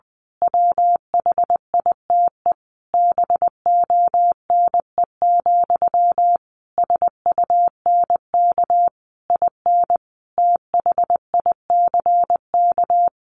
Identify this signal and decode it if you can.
Morse code, heard as a keyed tone, about 20 wpm, 700 Hz. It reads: WHITE BONE, SUNK IN THICK
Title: A Sound Of Thunder 20wpm